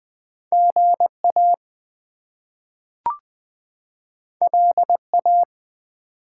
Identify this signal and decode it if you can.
Morse code, heard as a keyed tone, about 20 wpm, 700 Hz. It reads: GA LA
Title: US States and Canadian Provinces 20wpm